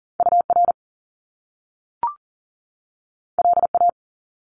Morse code, heard as a keyed tone, about 38 wpm, 700 Hz. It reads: UR LA
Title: Top Two Letter Combinations in Words 40wpm